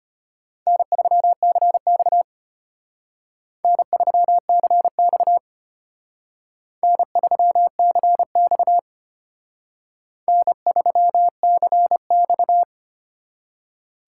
Morse code, heard as a keyed tone, about 32 wpm, 700 Hz. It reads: N3CX N3CX N3CX N3CX